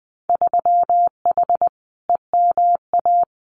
Morse code, heard as a keyed tone, about 20 wpm, 700 Hz. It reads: 3H EMA
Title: Winter Field Day Exchanges Encore 20wpm